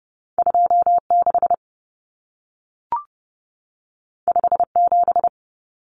Morse code, heard as a keyed tone, about 30 wpm, 700 Hz. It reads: 26 57
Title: Common Cut Numbers - 2-Digit 30wpm